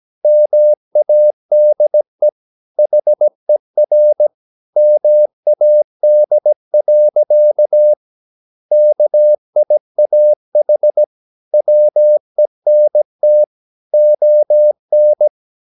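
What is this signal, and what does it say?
Morse code, heard as a keyed tone, about 17 wpm, 600 Hz. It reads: MADE HER MAD. KIAH WENT ON
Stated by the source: 17 wpm Morse practice sending the English text of a novel.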